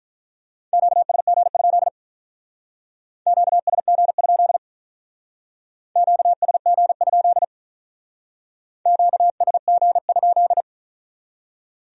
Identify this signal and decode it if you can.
Morse code, heard as a keyed tone, about 44 wpm, 700 Hz. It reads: QSG? QSG? QSG? QSG?
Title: Speed-Racing Q-Codes - 35wpm